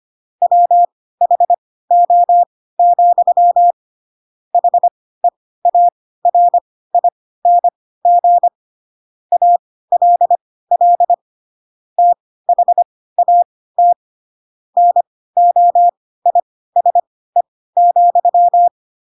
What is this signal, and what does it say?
Morse code, heard as a keyed tone, about 25 wpm, spaced as for 16 wpm, 700 Hz. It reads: WHO, HEARING ALL THAT NOISE,
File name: Pinocchio / Pinocchio ch3 10 25 ik1hge